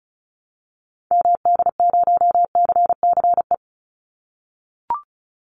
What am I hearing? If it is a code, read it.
Morse code, heard as a keyed tone, about 34 wpm, 700 Hz. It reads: MD0CCE
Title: Call Signs - 35wpm